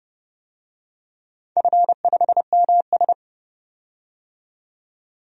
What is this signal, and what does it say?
Morse code, heard as a keyed tone, about 32 wpm, 700 Hz. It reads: F5MS
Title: DXCC-100 Call Signs - 4 Characters - 30wpm